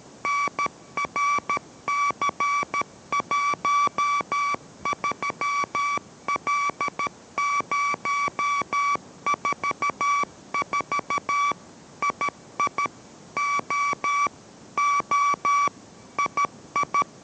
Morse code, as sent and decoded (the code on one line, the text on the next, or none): -. .-. -.-. .---- ...-- .-.. ----- ....- ....- / .. .. / --- / --- / .. ..
NRC13L044 II O O II